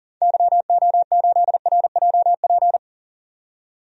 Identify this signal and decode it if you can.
Morse code, heard as a keyed tone, about 40 wpm, 700 Hz. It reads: YO8RJP
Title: DXCC-100 Call Signs - 6 Characters - 40wpm